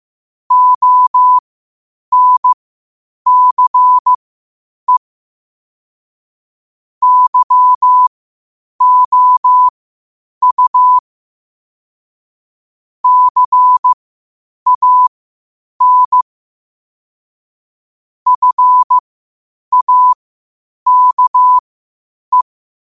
Morse code, heard as a keyed tone, about 15 wpm, spaced as for 8 wpm, 1000 Hz. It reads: ONCE YOU CAN FAKE
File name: qotdfiles / 8wpm